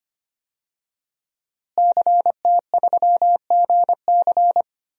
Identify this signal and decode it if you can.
Morse code, heard as a keyed tone, about 25 wpm, 700 Hz. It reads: CT3GC
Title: DXCC-100 Call Signs - 5 Characters - Repeated 3x - 25wpm